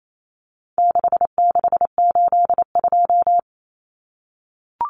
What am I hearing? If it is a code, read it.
Morse code, heard as a keyed tone, about 28 wpm, 700 Hz. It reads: 6682
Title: Numbers - 4-Digit 28wpm